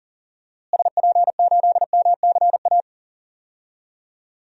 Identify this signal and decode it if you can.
Morse code, heard as a keyed tone, about 40 wpm, 700 Hz. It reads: SP8MCA
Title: DXCC-100 Call Signs - 6 Characters - Repeated 3x - 40wpm